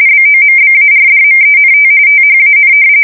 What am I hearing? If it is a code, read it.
RTTY, about 45.45 baud. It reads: BOMBARDMENT THAN
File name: fsk170 / battleshipnj